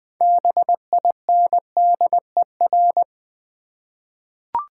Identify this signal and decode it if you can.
Morse code, heard as a keyed tone, about 20 wpm, 700 Hz. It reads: BINDER
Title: Difficult Words with Hints - 20wpm